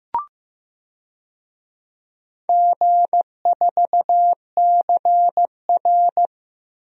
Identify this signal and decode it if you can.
Morse code, heard as a keyed tone, about 16 wpm, 700 Hz. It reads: G4CR
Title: DXCC-100 Call Signs - 4 Characters - Repeated 3x - 15wpm